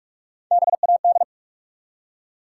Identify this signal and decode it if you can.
Morse code, heard as a keyed tone, about 44 wpm, 700 Hz. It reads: BAD
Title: Three Letter Words - 45wpm